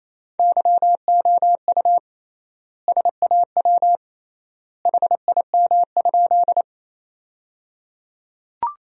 Morse code, heard as a keyed tone, about 28 wpm, 700 Hz. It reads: YOU SAW HIM?
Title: 3x Word-Spacing - Sentences from Top 300 Words - 28wpm